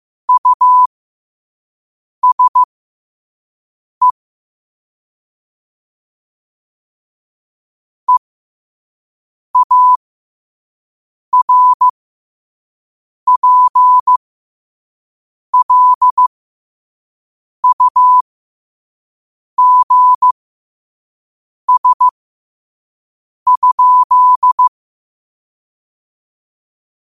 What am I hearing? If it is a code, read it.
Morse code, heard as a keyed tone, about 15 wpm, spaced as for 5 wpm, 1000 Hz. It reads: USE EARPLUGS?
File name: qotdfiles / 5wpm